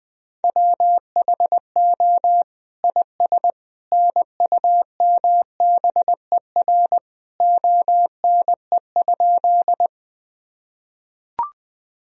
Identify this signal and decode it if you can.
Morse code, heard as a keyed tone, about 21 wpm, 700 Hz. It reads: WHO IS NUMBER ONE?
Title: Sentences from Top 1000 Words - Review Part 1 - 20wpm